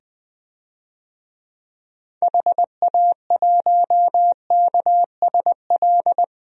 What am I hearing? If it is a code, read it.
Morse code, heard as a keyed tone, about 20 wpm, 700 Hz. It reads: HA1KSL
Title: DXCC-100 Call Signs - 6 Characters - 20wpm